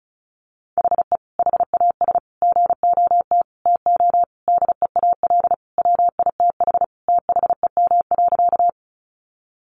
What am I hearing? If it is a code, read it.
Morse code, heard as a keyed tone, about 34 wpm, 700 Hz. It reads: HE HAS GOT TO DEAL WITH THEM.
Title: Sentences from Top 800 Words 35wpm